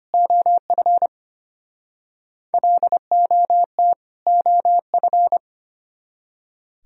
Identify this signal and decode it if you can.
Morse code, heard as a keyed tone, about 34 wpm, 700 Hz. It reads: OF LOT OF